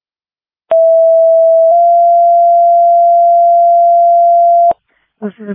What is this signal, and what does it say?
Two-tone page, heard as a one-second tone, then a three-second tone, about 660 then 690 Hz, with speech. It is heard from 0.7 to 4.7 s.